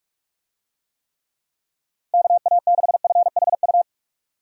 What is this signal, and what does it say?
Morse code, heard as a keyed tone, about 47 wpm, 700 Hz. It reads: KA6FHU
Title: USA Call Signs - 6 Characters - Repeated 3x - 45wpm